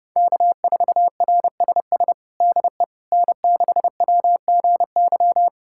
Morse code, heard as a keyed tone, about 30 wpm, 700 Hz. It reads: K4RSS DE N6WGY
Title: USA Callsign DE Callsign 30wpm